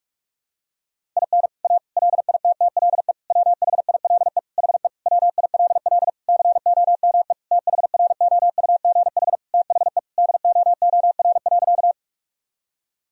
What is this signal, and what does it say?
Morse code, heard as a keyed tone, about 46 wpm, 700 Hz. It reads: IN A LITTLE WHILE HE WILL COME THROUGH THE DOOR.
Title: Sentences from Top 1000 Words - Review Part 1 - 45wpm